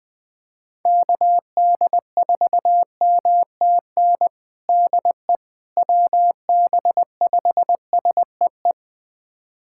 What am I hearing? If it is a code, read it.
Morse code, heard as a keyed tone, about 20 wpm, 700 Hz. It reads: KD4MTN DE WB5SEE
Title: USA Callsign DE Callsign - 6 Characters - 20wpm